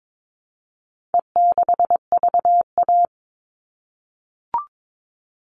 Morse code, heard as a keyed tone, about 22 wpm, 700 Hz. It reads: E6VA
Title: All Cut Numbers - 4-Digit 22wpm